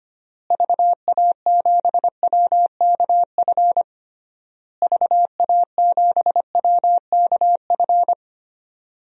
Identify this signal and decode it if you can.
Morse code, heard as a keyed tone, about 25 wpm, 700 Hz. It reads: VA7WKF VA7WKF
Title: DXCC-100 Call Signs - 6 Characters - Repeated 3x - 25wpm